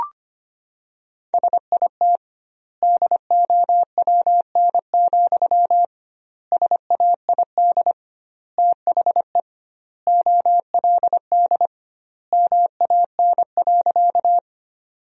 Morse code, heard as a keyed tone, about 25 wpm, 700 Hz. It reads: SIT DOWN, SAID THE OLD MAN.
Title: 2x Word-Spacing - Sentences from Top 500 Words - 25wpm